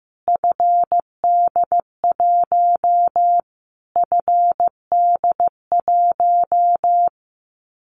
Morse code, heard as a keyed tone, about 15 wpm, 700 Hz. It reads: FD1 FD1
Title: Callsign Prefixes for Top 25 DXCC - Repeated 3x - 15wpm